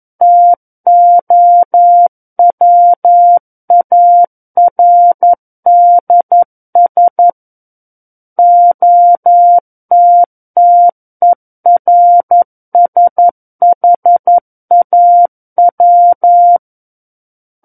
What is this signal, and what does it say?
Morse code, heard as a keyed tone, about 11 wpm, 700 Hz. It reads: TOWARDS OTTERSHAW
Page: War of the Worlds - 02-Chapter 2 - 11 WPM